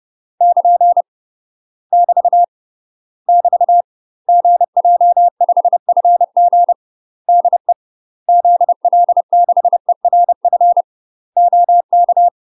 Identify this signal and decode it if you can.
Morse code, heard as a keyed tone, about 30 wpm, 700 Hz. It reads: <KN> <BT> <BT> GJ5FG DE ZL6ERF OK
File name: UK MASTER QSOs TRAINING / 30WPM / UK MASTER QSOs 0027